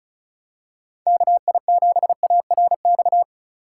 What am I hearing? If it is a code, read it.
Morse code, heard as a keyed tone, about 36 wpm, 700 Hz. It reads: KI7ARX
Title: USA Call Signs - 6 Characters - 2x Character Spacing - 35wpm